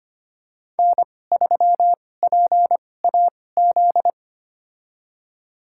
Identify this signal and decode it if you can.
Morse code, heard as a keyed tone, about 25 wpm, spaced as for 18 wpm, 700 Hz. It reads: N3PAZ
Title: USA Call Signs - 5 Characters - 2x Character Spacing - 25wpm